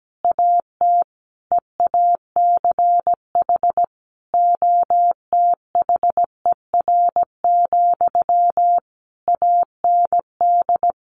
Morse code, heard as a keyed tone, about 17 wpm, 700 Hz. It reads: AT EACH OTHER, AND
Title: A Sound Of Thunder 17wpm